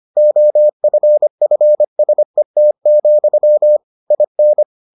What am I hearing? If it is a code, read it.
Morse code, heard as a keyed tone, about 25 wpm, 600 Hz. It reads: OFFSET, IN